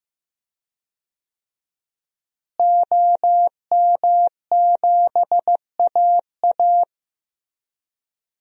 Morse code, heard as a keyed tone, about 15 wpm, 700 Hz. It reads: OM7AA
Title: DXCC-100 Call Signs - 5 Characters - Repeated 3x - 15wpm